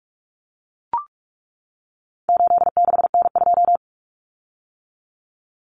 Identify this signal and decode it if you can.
Morse code, heard as a keyed tone, about 44 wpm, 700 Hz. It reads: 86N2
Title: Common Cut Numbers - 4-Digit 45wpm